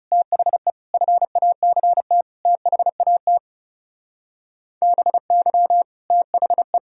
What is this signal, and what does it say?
Morse code, heard as a keyed tone, about 36 wpm, 700 Hz. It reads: THE FACT THAT BY THE
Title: Speed-Racing - Sets of 4 Words 30wpm